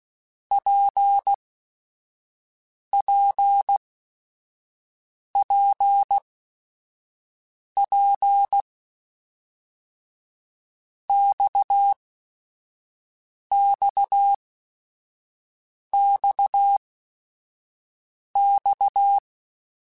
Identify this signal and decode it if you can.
Morse code, heard as a keyed tone, about 16 wpm, spaced as for 5 wpm, 800 Hz. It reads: PPPP XXXX
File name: Basic Sets / Basic Letter Groups Collective7 / RKPX@5